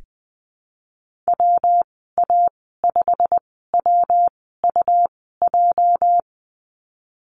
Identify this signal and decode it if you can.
Morse code, heard as a keyed tone, about 19 wpm, 700 Hz. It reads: W A 5 W U J
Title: 238 Lesson 38g - Letter q - Call Signs 20wpm